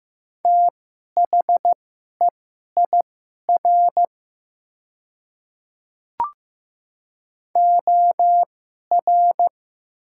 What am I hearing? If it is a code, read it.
Morse code, heard as a keyed tone, about 15 wpm, spaced as for 11 wpm, 700 Hz. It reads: THEIR OR
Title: 2x Character Spacing - Top 200 Words - 15wpm